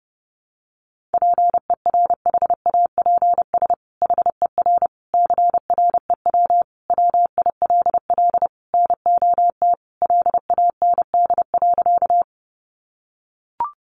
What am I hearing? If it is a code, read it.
Morse code, heard as a keyed tone, about 30 wpm, 700 Hz. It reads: PERHAPS HER CREW WILL NOT LAND.
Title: Sentences from Top 1500 Words 30wpm